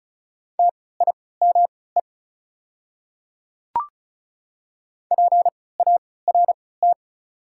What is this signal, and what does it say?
Morse code, heard as a keyed tone, about 36 wpm, spaced as for 20 wpm, 690 Hz. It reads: TIME PART
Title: 3x Character Spacing - Top 500 Words - 35wpm